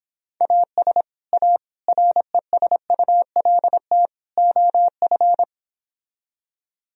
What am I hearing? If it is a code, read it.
Morse code, heard as a keyed tone, about 26 wpm, 700 Hz. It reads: AS A RESULT OF